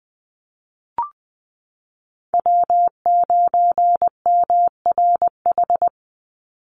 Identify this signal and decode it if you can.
Morse code, heard as a keyed tone, about 20 wpm, 700 Hz. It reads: W9MRH
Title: NA SOTA Chaser Callsigns 20wpm